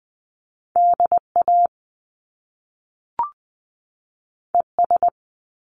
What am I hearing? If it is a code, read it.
Morse code, heard as a keyed tone, about 20 wpm, 700 Hz. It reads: D A E S
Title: Top Two Letter Combinations in Words 20wpm